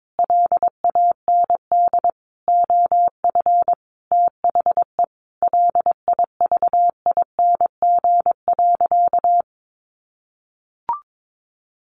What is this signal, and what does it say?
Morse code, heard as a keyed tone, about 22 wpm, 700 Hz. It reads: LAND OF THE LIVING.
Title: Sentences from Top 1100 Words 22wpm